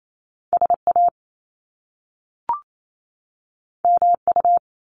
Morse code, heard as a keyed tone, about 28 wpm, 700 Hz. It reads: SA MU
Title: Top Two Letter Combinations in Words 28wpm